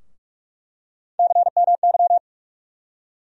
Morse code, heard as a keyed tone, about 46 wpm, 700 Hz. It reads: CMY
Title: USA Call Sign Postfixes - 3 Letters 45wpm